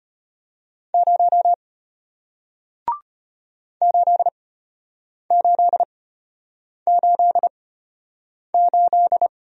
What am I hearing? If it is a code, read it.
Morse code, heard as a keyed tone, about 31 wpm, spaced as for 8 wpm, 690 Hz. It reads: 0 8888
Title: Speed-Racing Single Letter-Number 25wpm